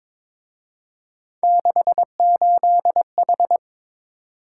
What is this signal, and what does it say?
Morse code, heard as a keyed tone, about 27 wpm, 700 Hz. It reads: 68H
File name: Speed-Racing - Three Letter-Number Combinations - 22wpm